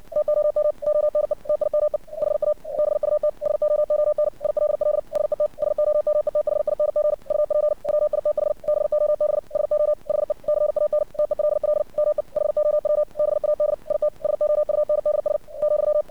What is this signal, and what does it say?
Morse code, heard as a keyed tone, about 60 wpm, 600 Hz. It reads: TOM ONE TIME HA BUT SOON ILL SET SOMETHING UP WITH BOB SO HE CAN TELL ME HOW BAD IT SOUNDS ,
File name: archived / CRSnet